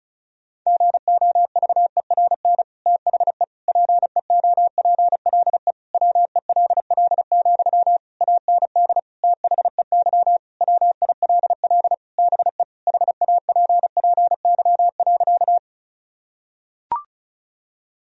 Morse code, heard as a keyed tone, about 34 wpm, 700 Hz. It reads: GOVERN THE PEOPLE WELL, AND THEY WILL BE HAPPY.
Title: Sentences from Top 1000 Words - Review Part 2 - 35wpm